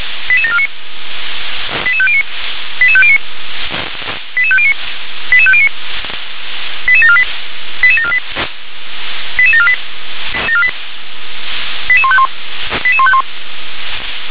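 Five-tone selective call (ZVEI), heard as quick runs of five tones, tones of about 70 ms. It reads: 88749 88509 88509 88509 88509 88748 88748 88748 88151 88151
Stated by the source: ZVEI five-tone call received on an SDR.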